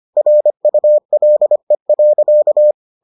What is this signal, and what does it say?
Morse code, heard as a keyed tone, about 25 wpm, 600 Hz. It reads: RULE.